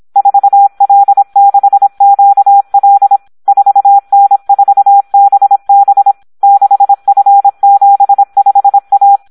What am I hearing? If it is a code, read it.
Morse code, heard as a keyed tone, about 26 wpm, 800 Hz. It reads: 4L6QL 4N4BB 6F75A
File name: Audio practica CW / SD 1 26